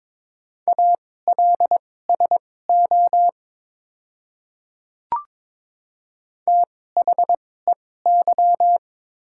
Morse code, heard as a keyed tone, about 22 wpm, spaced as for 16 wpm, 700 Hz. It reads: ALSO THEY
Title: 2x Character Spacing - Top 300 Words - 22wpm